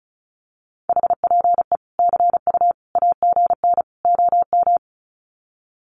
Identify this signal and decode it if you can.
Morse code, heard as a keyed tone, about 34 wpm, 700 Hz. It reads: HPE CU AGN OM
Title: Common QSO Phrases 35wpm